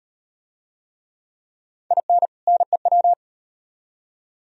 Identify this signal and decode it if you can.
Morse code, heard as a keyed tone, about 38 wpm, 700 Hz. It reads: IN NEW